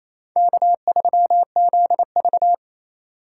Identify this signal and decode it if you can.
Morse code, heard as a keyed tone, about 28 wpm, 700 Hz. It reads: K3ZV
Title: USA Call Signs - 4 Characters - 2x Character Spacing - 28wpm